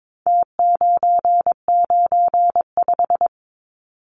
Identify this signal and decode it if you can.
Morse code, heard as a keyed tone, about 22 wpm, 700 Hz. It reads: T995
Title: Numbers - Contest Style - 4-Digit 22wpm